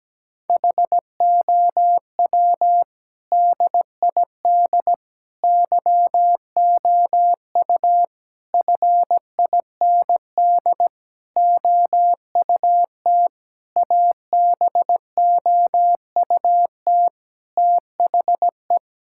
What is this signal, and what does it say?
Morse code, heard as a keyed tone, about 17 wpm, 700 Hz. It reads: HOW DID YOU FIND OUT ABOUT THE
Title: Sentences from Top 1000 Words - Review Part 5 - 17wpm